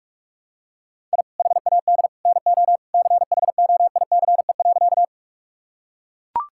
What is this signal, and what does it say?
Morse code, heard as a keyed tone, about 44 wpm, 690 Hz. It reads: I HAD NO CHOICE.
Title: Sentences from Top 1000 Words - Review Part 6 - 45wpm